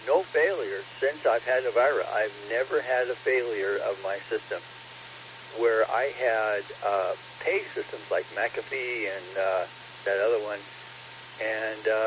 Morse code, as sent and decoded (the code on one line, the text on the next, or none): none